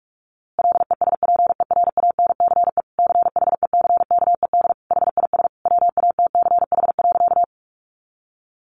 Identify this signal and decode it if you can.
Morse code, heard as a keyed tone, about 44 wpm, 700 Hz. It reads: LESPERANCE CHECKED HIS WATCH.
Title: A Sound Of Thunder 45wpm